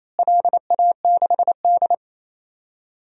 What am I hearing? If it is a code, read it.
Morse code, heard as a keyed tone, about 28 wpm, 700 Hz. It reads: LA6D
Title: DXCC-100 Call Signs - 4 Characters - Repeated 3x - 28wpm